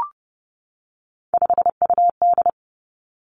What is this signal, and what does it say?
Morse code, heard as a keyed tone, about 31 wpm, 700 Hz. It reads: 5UD
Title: Three Challenging Letter-Number Combinations 30wpm